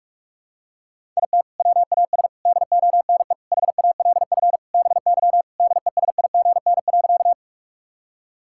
Morse code, heard as a keyed tone, about 46 wpm, 700 Hz. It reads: IT WAS DONE HALF BY DESIGN.
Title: Sentences from Top 1000 Words - Review Part 5 - 45wpm